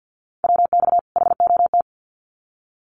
Morse code, heard as a keyed tone, about 50 wpm, 700 Hz. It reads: WX HOT
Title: Common QSO Phrases 50wpm